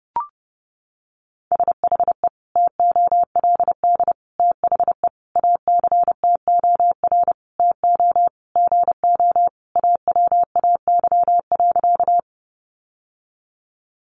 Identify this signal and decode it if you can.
Morse code, heard as a keyed tone, about 30 wpm, 700 Hz. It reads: SHE TOLD THE ACTOR TO GO AWAY.
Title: Sentences from Top 1500 Words 30wpm